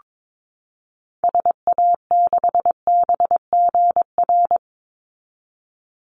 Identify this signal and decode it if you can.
Morse code, heard as a keyed tone, about 22 wpm, 700 Hz. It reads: SA6BGR
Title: Call Signs - 22wpm